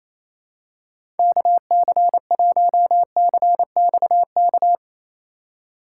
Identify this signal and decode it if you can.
Morse code, heard as a keyed tone, about 28 wpm, 700 Hz. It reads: KC1CXK
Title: USA Call Signs - 4x Character Spacing - 28wpm